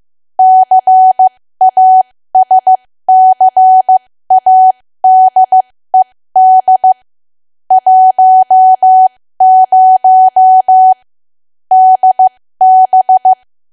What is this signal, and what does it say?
Morse code, heard as a keyed tone, about 15 wpm, spaced as for 13 wpm, 750 Hz. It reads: CASCADED 10 DB
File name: cwa / W1AW-Level2 / 13WPM